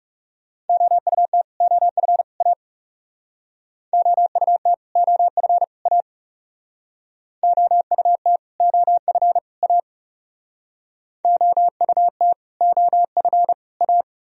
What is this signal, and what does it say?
Morse code, heard as a keyed tone, about 35 wpm, 700 Hz. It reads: OUT OF A OUT OF A OUT OF A OUT OF A
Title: Speed-Racing - Sets of 3 Words 30wpm